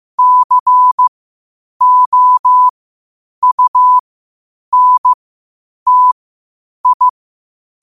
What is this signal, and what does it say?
Morse code, heard as a keyed tone, about 15 wpm, spaced as for 8 wpm, 1000 Hz. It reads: COUNTI
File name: qotdfiles / 8wpm